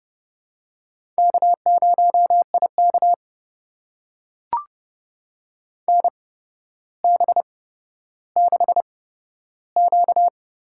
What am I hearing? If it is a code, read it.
Morse code, heard as a keyed tone, about 30 wpm, spaced as for 13 wpm, 700 Hz. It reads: K0IK NB6Q
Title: USA Call Signs - 4 Characters - 8x Character Spacing - 30wpm